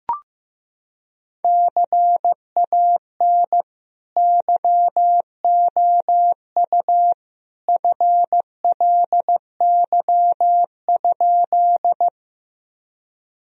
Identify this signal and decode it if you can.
Morse code, heard as a keyed tone, about 15 wpm, 700 Hz. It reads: CAN YOU FLY?
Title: Sentences from Top 1000 Words - Review Part 2 - 15wpm